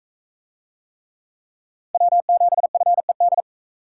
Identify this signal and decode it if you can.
Morse code, heard as a keyed tone, about 44 wpm, 700 Hz. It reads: W7FED